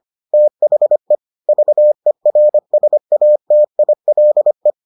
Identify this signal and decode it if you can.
Morse code, heard as a keyed tone, about 25 wpm, 600 Hz. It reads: THE VERSATILE